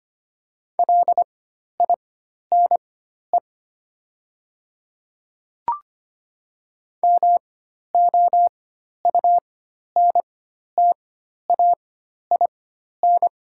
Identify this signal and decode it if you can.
Morse code, heard as a keyed tone, about 25 wpm, spaced as for 12 wpm, 690 Hz. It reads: LINE MOUNTAIN
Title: 4x Character Spacing - Top 500 Words - 25wpm